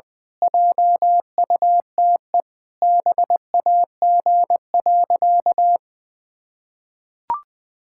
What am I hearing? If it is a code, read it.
Morse code, heard as a keyed tone, about 21 wpm, 700 Hz. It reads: JUTE BAG.
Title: Pangram Challenge v2 - 20wpm